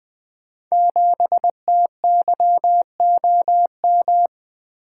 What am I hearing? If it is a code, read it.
Morse code, heard as a keyed tone, about 20 wpm, 700 Hz. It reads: 7TYOM
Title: Code Groups - 5 Letter:Numbers - 20wpm